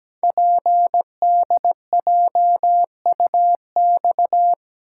Morse code, heard as a keyed tone, about 17 wpm, 700 Hz. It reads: PDJUX
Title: Code Groups - 5 Letters - 17wpm